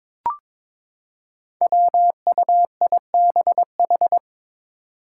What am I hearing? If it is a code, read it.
Morse code, heard as a keyed tone, about 22 wpm, 700 Hz. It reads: WUIBH
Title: Five Challenging Letter-Number Combinations 22wpm